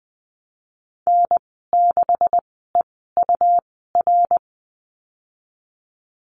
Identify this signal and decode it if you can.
Morse code, heard as a keyed tone, about 20 wpm, spaced as for 15 wpm, 700 Hz. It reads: N6EUR
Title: 186 Lesson 30g - Letter b - Call Signs 20wpm